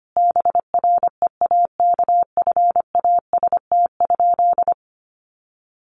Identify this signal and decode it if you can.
Morse code, heard as a keyed tone, about 25 wpm, 700 Hz. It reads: BREAKFAST?
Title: Sentences from Top 1500 Words 25wpm